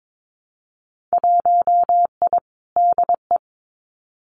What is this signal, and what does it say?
Morse code, heard as a keyed tone, about 22 wpm, 700 Hz. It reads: 1I DE
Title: Winter Field Day Exchanges Encore 22wpm